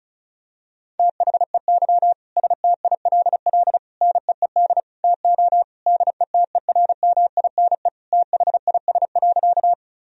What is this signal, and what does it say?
Morse code, heard as a keyed tone, about 36 wpm, 700 Hz. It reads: THEY STILL NEED TO DETERMINE THIS.
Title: Sentences from Top 1000 Words - Review Part 4 - 35wpm